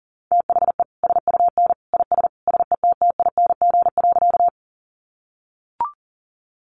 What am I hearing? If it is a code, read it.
Morse code, heard as a keyed tone, about 40 wpm, 700 Hz. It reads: THE SUN IS SETTING.
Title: Sentences from Top 1300 Words 40wpm